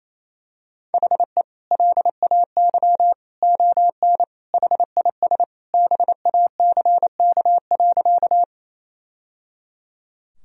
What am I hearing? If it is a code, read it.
Morse code, heard as a keyed tone, about 28 wpm, 700 Hz. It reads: HE LAY ON HIS BACK.
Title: Sentences from Top 1000 Words - Review Part 2 - 28wpm